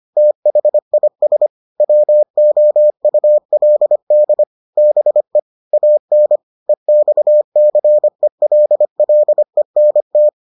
Morse code, heard as a keyed tone, about 25 wpm, 600 Hz. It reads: THIS WOULD BE AN EXCELLENT